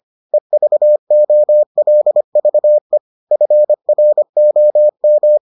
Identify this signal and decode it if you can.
Morse code, heard as a keyed tone, about 25 wpm, 600 Hz. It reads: EVOLVE FROM